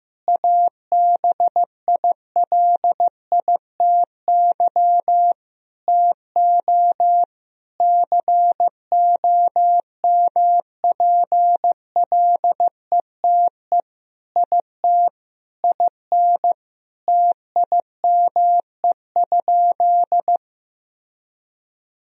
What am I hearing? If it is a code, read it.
Morse code, heard as a keyed tone, about 15 wpm, 700 Hz. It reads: ABILITY TO COMPLETE IT IN TIME?
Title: Sentences from Top 1000 Words - Review Part 3 - 15wpm